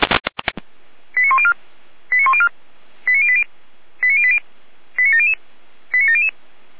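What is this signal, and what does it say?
Five-tone selective call (ZVEI), heard as quick runs of five tones, tones of about 70 ms. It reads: 89104 89104 89080 89080 89700 89700
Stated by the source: Decoded by SDR as ZVEI five-tone selective call.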